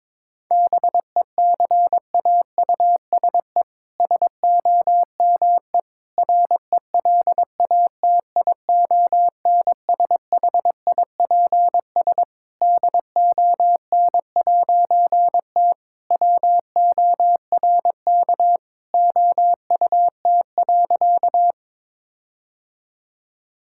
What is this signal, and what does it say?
Morse code, heard as a keyed tone, about 22 wpm, 700 Hz. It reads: BECAUSE SOME RELATIONSHIPS DON'T WORK OUT.
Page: morse-code-ninja-practice-22wpm